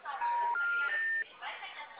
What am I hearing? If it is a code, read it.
Special information tone, with speech.
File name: out- -2003- - - .809